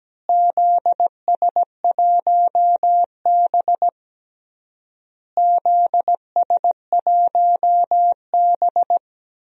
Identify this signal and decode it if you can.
Morse code, heard as a keyed tone, about 17 wpm, 700 Hz. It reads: ZS1B ZS1B
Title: DXCC-100 Call Signs - 4 Characters - Repeated 3x - 17wpm